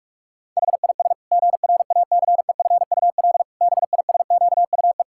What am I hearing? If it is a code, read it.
Morse code, heard as a keyed tone, about 41 wpm, 700 Hz. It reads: HIS GRACEFUL BISQUE
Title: Pangram Challenge v2 - 45wpm